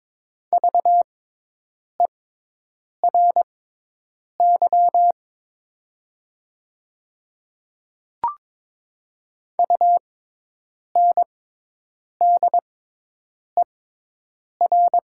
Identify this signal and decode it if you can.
Morse code, heard as a keyed tone, about 22 wpm, spaced as for 8 wpm, 700 Hz. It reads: VERY UNDER
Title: 6x Character Spacing - Top 200 Words - 22wpm